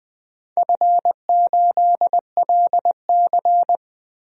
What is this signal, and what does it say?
Morse code, heard as a keyed tone, about 20 wpm, 700 Hz. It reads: F8LC
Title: DXCC-100 Call Signs - 4 Characters - 20wpm